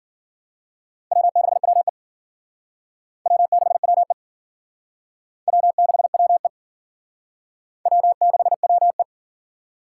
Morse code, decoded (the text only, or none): W6WE W6WE W6WE W6WE